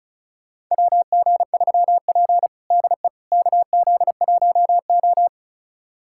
Morse code, heard as a keyed tone, about 36 wpm, 700 Hz. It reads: WG3P DE KZ1O
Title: USA Callsign DE Callsign - 4 Characters - 35wpm